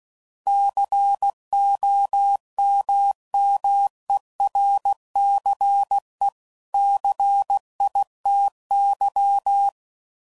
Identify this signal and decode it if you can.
Morse code, heard as a keyed tone, about 16 wpm, 800 Hz. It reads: COMMERCE CITY
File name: CW Code Groups @15WPM / fromSM's / Cities